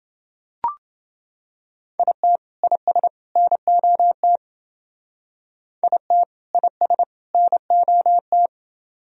Morse code, heard as a keyed tone, about 28 wpm, 700 Hz. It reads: IT IS NOT IT IS NOT